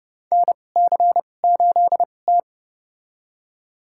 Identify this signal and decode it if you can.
Morse code, heard as a keyed tone, about 30 wpm, spaced as for 22 wpm, 700 Hz. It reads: NC8T
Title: USA Call Signs - 4 Characters - 2x Character Spacing - 30wpm